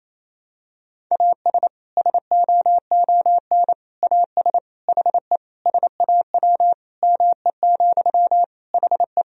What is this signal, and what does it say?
Morse code, heard as a keyed tone, about 27 wpm, 700 Hz. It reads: AS SOON AS HE SAW ME, HE
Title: Sentences from Top 1000 Words - Review Part 2 - 28wpm